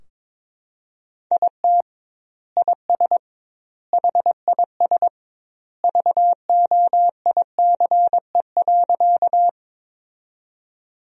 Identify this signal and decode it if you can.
Morse code, heard as a keyed tone, about 22 wpm, 700 Hz. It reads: IT IS HIS VOICE.
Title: 2x Word-Spacing - Sentences from Top 500 Words - 22wpm